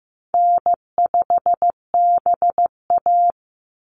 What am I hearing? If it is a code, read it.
Morse code, heard as a keyed tone, about 15 wpm, 700 Hz. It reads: N5BA
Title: Easy Call Signs Repeated 3x - 15wpm